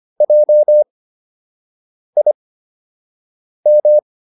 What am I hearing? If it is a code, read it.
Morse code, heard as a keyed tone, about 25 wpm, spaced as for 6 wpm, 600 Hz. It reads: JIM